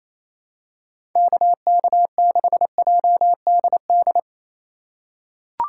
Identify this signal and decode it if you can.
Morse code, heard as a keyed tone, about 28 wpm, 700 Hz. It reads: KK6JDD
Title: USA Call Signs - 6 Characters - 3x Character Spacing - 28wpm